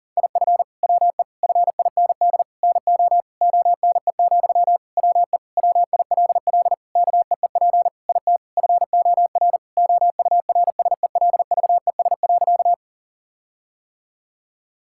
Morse code, heard as a keyed tone, about 40 wpm, 700 Hz. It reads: IF WE FIND NO ONE, WE WILL KEEP IT FOR OURSELVES.
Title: Sentences from Top 1600 Words 40wpm